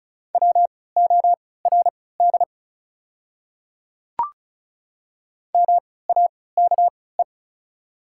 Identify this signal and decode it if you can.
Morse code, heard as a keyed tone, about 36 wpm, spaced as for 20 wpm, 700 Hz. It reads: WORD MAKE
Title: 3x Character Spacing - Top 200 Words - 35wpm